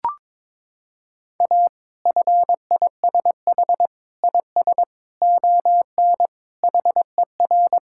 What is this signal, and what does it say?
Morse code, heard as a keyed tone, about 22 wpm, 700 Hz. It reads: A FISH IS ON HER
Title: Sentences from Top 1600 Words 22wpm